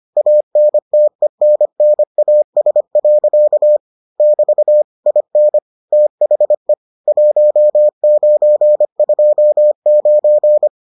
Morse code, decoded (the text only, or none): ANTENNAS. <BT> IN THE 1929